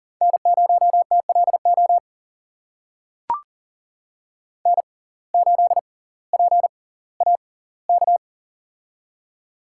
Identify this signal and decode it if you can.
Morse code, heard as a keyed tone, about 40 wpm, spaced as for 20 wpm, 700 Hz. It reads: N0TLO N8PAK
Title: USA Call Signs - 5 Characters - 6x Character Spacing - 40wpm